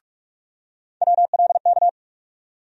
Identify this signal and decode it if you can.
Morse code, heard as a keyed tone, about 46 wpm, 700 Hz. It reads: WLK
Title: USA Call Sign Postfixes - 3 Letters 45wpm